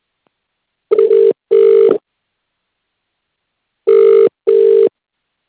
Ringback tone (UK), 400 + 450 Hz.